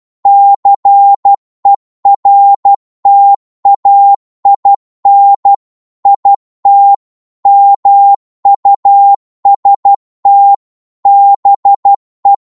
Morse code, heard as a keyed tone, about 12 wpm, 800 Hz. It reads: CERTAIN IT MUST BE
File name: Morse / psg-12wpm